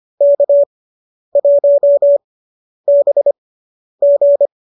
Morse code, heard as a keyed tone, about 25 wpm, spaced as for 10 wpm, 590 Hz. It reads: K1BG